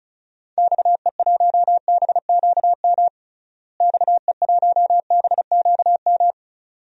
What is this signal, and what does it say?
Morse code, heard as a keyed tone, about 34 wpm, 700 Hz. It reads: XE1BQM XE1BQM
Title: DXCC-100 Call Signs - 6 Characters - Repeated 2x - 35wpm